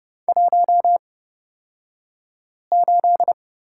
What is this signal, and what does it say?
Morse code, heard as a keyed tone, about 31 wpm, spaced as for 5 wpm, 700 Hz. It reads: 18
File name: Lesson 36d - BT - Two Characters 30wpm